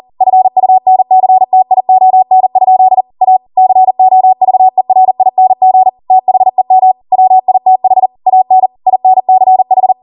Morse code, heard as a keyed tone, about 40 wpm, 750 Hz. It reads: FUNCTION? A COVERING THEM WITH AN INCH
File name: Morse / 40WPM